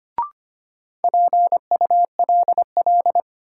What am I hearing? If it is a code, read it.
Morse code, heard as a keyed tone, about 25 wpm, 700 Hz. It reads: PULL
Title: Sentences from Top 400 Words - New Word Drill - 25wpm